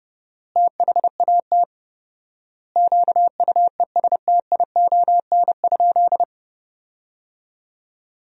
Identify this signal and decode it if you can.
Morse code, heard as a keyed tone, about 30 wpm, 700 Hz. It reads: THAT QUESTION?
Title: 4x Word-Spacing - Sentences from Top 400 Words - 30wpm